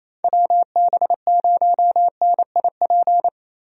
Morse code, heard as a keyed tone, about 28 wpm, 700 Hz. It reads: WB0NIP
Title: USA Call Signs - 6 Characters - Repeated 2x - 28wpm